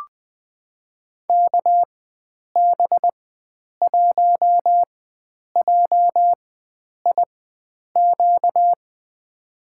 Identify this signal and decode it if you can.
Morse code, heard as a keyed tone, about 20 wpm, spaced as for 9 wpm, 700 Hz. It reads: KB1JIQ
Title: USA Call Signs - 6 Characters - 4x Character Spacing - 20wpm